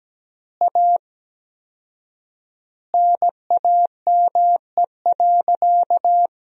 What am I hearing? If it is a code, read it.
Morse code, heard as a keyed tone, about 17 wpm, 700 Hz. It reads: A NAME.
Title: 4x Word-Spacing - Sentences from Top 500 Words - 17wpm